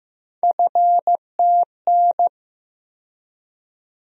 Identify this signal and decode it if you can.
Morse code, heard as a keyed tone, about 15 wpm, 700 Hz. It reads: FTN
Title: USA Call Sign Postfixes - 3 Letters 15wpm